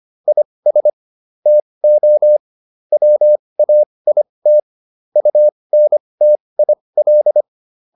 Morse code, heard as a keyed tone, about 25 wpm, spaced as for 20 wpm, 600 Hz. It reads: IS TO WAIT UNTIL